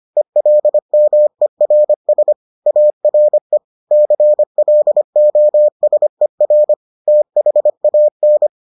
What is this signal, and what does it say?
Morse code, heard as a keyed tone, about 25 wpm, 600 Hz. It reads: ELMERS ARE CLOSER THAN